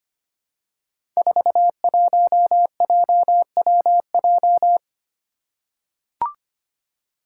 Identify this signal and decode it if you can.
Morse code, heard as a keyed tone, about 25 wpm, 700 Hz. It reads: 41JWJ
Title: Five Challenging Letter-Number Combinations 25wpm